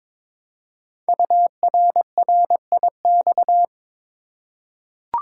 Morse code, heard as a keyed tone, about 22 wpm, 700 Hz. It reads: URRIX
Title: Code Groups - 5 Letters - 22wpm